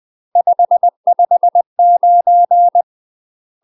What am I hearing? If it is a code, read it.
Morse code, heard as a keyed tone, about 20 wpm, 700 Hz. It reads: 559
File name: UK MASTER QSOs TRAINING/20WPM